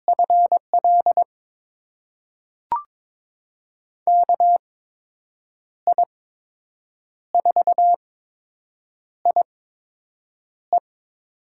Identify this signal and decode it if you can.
Morse code, heard as a keyed tone, about 22 wpm, spaced as for 7 wpm, 700 Hz. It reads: FL KI4IE
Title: USA Call Signs - 5 Characters - 8x Character Spacing - 22wpm